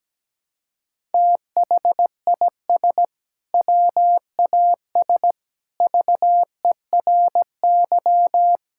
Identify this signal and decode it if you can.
Morse code, heard as a keyed tone, about 17 wpm, 700 Hz. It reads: THIS WAS VERY
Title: Sentences from Top 1000 Words - Review Part 6 - 17wpm